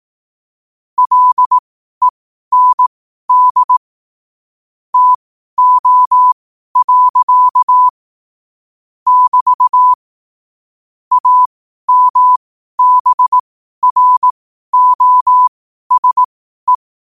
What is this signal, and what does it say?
Morse code, heard as a keyed tone, about 18 wpm, spaced as for 12 wpm, 1000 Hz. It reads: LEND TO. <BT> AMBROSE
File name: 12wpm